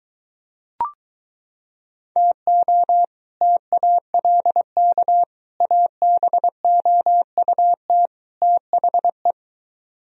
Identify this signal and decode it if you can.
Morse code, heard as a keyed tone, about 23 wpm, 700 Hz. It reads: TO TALK ABOUT THE